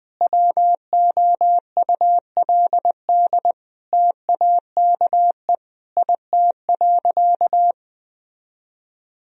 Morse code, heard as a keyed tone, about 20 wpm, 700 Hz. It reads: WOULD TAKE IT.
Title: Sentences from Top 1000 Words - Review Part 4 - 20wpm